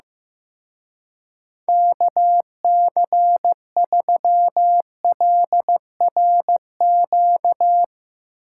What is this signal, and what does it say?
Morse code, heard as a keyed tone, about 17 wpm, 700 Hz. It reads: KC3LRQ
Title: Speed Racing - USA 2x3 Call Signs 15wpm